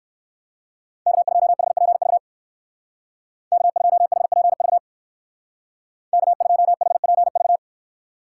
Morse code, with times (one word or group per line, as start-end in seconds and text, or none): 1.1-2.2 K2HPV
3.5-4.8 K2HPV
6.1-7.6 K2HPV